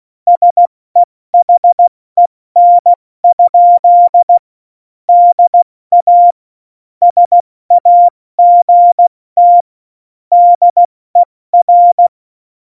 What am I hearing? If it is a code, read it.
Morse code, heard as a keyed tone, about 16 wpm, spaced as for 14 wpm, 700 Hz. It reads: SEHEN? DA SAGT DER
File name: Morsekurs / Witze / Witz 27. November 2018, 70 F80